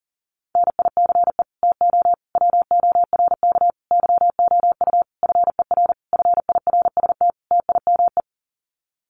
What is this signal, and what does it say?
Morse code, heard as a keyed tone, about 40 wpm, 700 Hz. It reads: NICE TO WORK YOU FER FIRST TIME
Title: Common QSO Phrases 40wpm